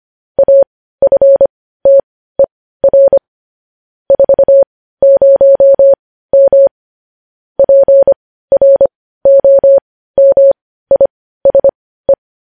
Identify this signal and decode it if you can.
Morse code, heard as a keyed tone, about 25 wpm, spaced as for 15 wpm, 550 Hz. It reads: AFTER 40M PROMISE
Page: news headlines in morse code at 15 wpm